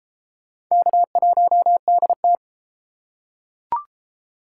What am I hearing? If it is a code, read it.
Morse code, heard as a keyed tone, about 32 wpm, 700 Hz. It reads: K1DT